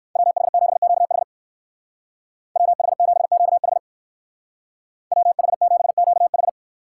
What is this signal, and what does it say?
Morse code, heard as a keyed tone, about 57 wpm, 700 Hz. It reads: WH7QH WH7QH WH7QH